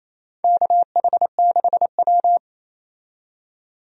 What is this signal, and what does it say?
Morse code, heard as a keyed tone, about 27 wpm, 690 Hz. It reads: KH6W
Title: DXCC-100 Call Signs - 4 Characters - 28wpm